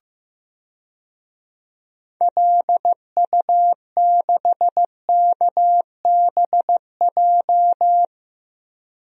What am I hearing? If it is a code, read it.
Morse code, heard as a keyed tone, about 15 wpm, 700 Hz. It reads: LU6KBJ
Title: DXCC-100 Call Signs - 6 Characters - Repeated 3x - 15wpm